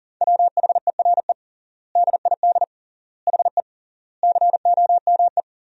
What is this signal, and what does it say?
Morse code, heard as a keyed tone, about 40 wpm, 700 Hz. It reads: WHERE DID HE COME
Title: 3x Word-Spacing - Sentences from Top 200 Words - 40wpm